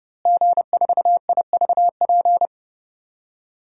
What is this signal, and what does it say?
Morse code, heard as a keyed tone, about 30 wpm, 700 Hz. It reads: G4IVP
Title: DXCC-100 Call Signs - 5 Characters - Repeated 3x - 30wpm